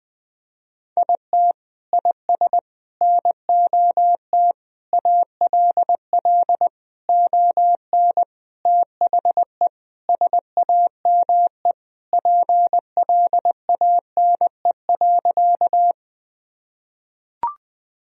Morse code, heard as a keyed tone, about 21 wpm, 700 Hz. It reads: IT IS NOT ALL ON THE SAME PLANE.
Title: Sentences from Top 1000 Words - Review Part 3 - 20wpm